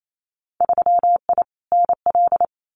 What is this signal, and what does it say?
Morse code, heard as a keyed tone, about 28 wpm, 700 Hz. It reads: 3I NL
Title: Winter Field Day Exchanges 28wpm